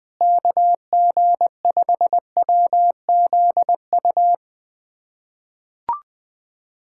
Morse code, heard as a keyed tone, about 20 wpm, 700 Hz. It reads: KG5WZU
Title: USA Callsign DE Callsign - 6 Characters - 20wpm